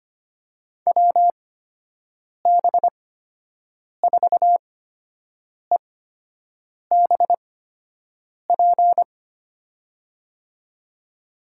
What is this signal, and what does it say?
Morse code, heard as a keyed tone, about 25 wpm, spaced as for 7 wpm, 700 Hz. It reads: WB4EBP
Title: USA Call Signs - 8x Character Spacing - 25wpm